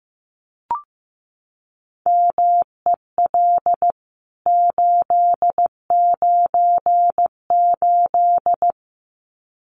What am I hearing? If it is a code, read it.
Morse code, heard as a keyed tone, about 15 wpm, 700 Hz. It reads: MEL 898
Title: CWT Exchange 15wpm